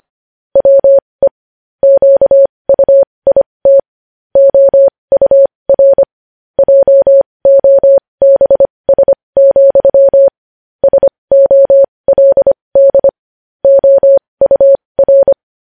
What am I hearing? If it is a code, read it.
Morse code, heard as a keyed tone, about 25 wpm, spaced as for 20 wpm, 550 Hz. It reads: WE QUIT OUR JOBS, SOLD OUR
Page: News Headlines in Morse Code at 20 WPM